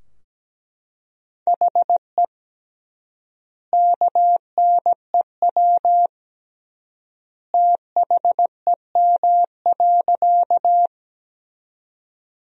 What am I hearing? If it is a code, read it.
Morse code, heard as a keyed tone, about 17 wpm, 700 Hz. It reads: HE KNEW THEM.
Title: 3x Word-Spacing - Sentences from Top 400 Words - 17wpm